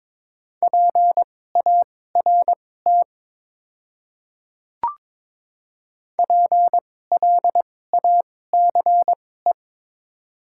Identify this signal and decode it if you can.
Morse code, heard as a keyed tone, about 22 wpm, spaced as for 16 wpm, 700 Hz. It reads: PART PLACE
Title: 2x Character Spacing - Top 300 Words - 22wpm